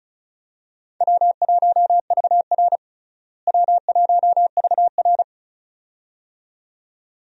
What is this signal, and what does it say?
Morse code, heard as a keyed tone, about 34 wpm, 700 Hz. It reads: W1VR W1VR
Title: USA Call Signs - 4 Characters - Repeated 2x - 35wpm